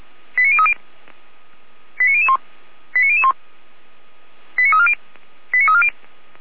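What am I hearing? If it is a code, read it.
Five-tone selective call (ZVEI), heard as quick runs of five tones, tones of about 70 ms. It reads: 89030 89001 89001 89340 89340
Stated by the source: Five-tone ZVEI selective call received on an SDR.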